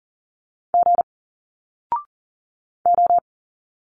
Morse code, heard as a keyed tone, about 40 wpm, 700 Hz. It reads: G O
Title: Single Letters 40wpm